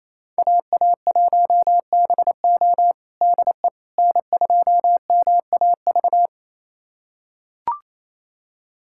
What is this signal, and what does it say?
Morse code, heard as a keyed tone, about 28 wpm, 700 Hz. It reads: AA1BO DE N2MAV
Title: USA Callsign DE Callsign - 5 Characters - 28wpm